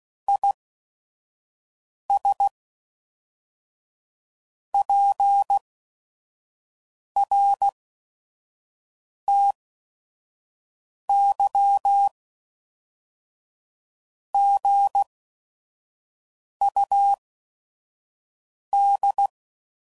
Morse code, heard as a keyed tone, about 16 wpm, spaced as for 5 wpm, 800 Hz. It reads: IS PRTY GUD
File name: CW Code Groups @5WPM / QSOs / QSO WA7QOX 1